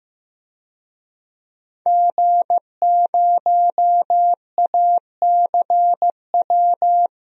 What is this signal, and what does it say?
Morse code, heard as a keyed tone, about 15 wpm, 700 Hz. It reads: G0ACW
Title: DXCC-100 Call Signs - 5 Characters - Repeated 3x - 15wpm